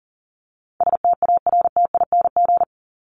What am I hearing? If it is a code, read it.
Morse code, heard as a keyed tone, about 40 wpm, 700 Hz. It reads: STARTING
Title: ING Suffix - Top 100 Words with Base Words 40wpm